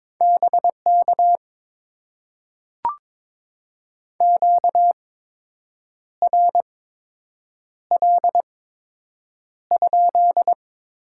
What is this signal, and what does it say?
Morse code, heard as a keyed tone, about 22 wpm, spaced as for 7 wpm, 700 Hz. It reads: BK QRL?
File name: Top 90 QSO Elements - 8x Character Spacing - 22wpm